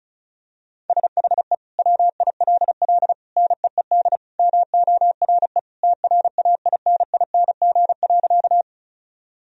Morse code, heard as a keyed tone, about 36 wpm, 700 Hz. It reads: SHE WILL NEED MORE TRAINING.
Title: Sentences from Top 1000 Words - Review Part 4 - 35wpm